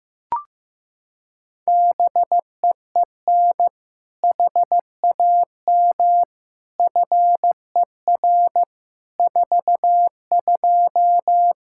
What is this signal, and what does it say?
Morse code, heard as a keyed tone, about 16 wpm, 700 Hz. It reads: BEEN HAM FER 42
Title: Practice QSO - Exchange 3 - More Info - 15wpm